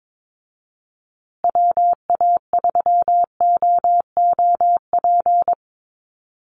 Morse code, heard as a keyed tone, about 22 wpm, 700 Hz. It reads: WA3OOP
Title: USA Call Signs Repeated 3x 22wpm